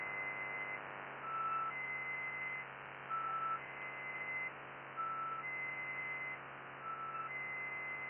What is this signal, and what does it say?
Emergency Alert System (SAME) tones (header bursts), heard as harsh buzzing bursts. It is heard from 0.9 to 7.8 s.